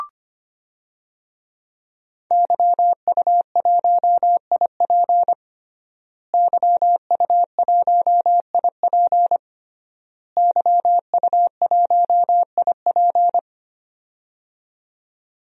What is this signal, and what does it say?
Morse code, heard as a keyed tone, about 25 wpm, 700 Hz. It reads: YU1IP YU1IP YU1IP
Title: DXCC-100 Call Signs - 5 Characters - Repeated 3x - 25wpm